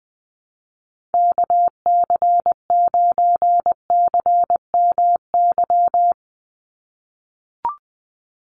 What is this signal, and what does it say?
Morse code, heard as a keyed tone, about 19 wpm, 700 Hz. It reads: KC9CMY
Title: USA Call Signs 20wpm